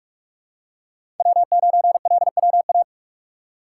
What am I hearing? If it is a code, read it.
Morse code, heard as a keyed tone, about 42 wpm, 700 Hz. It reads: W9LWA
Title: USA Call Signs - 5 Characters - 3x Character Spacing - 45wpm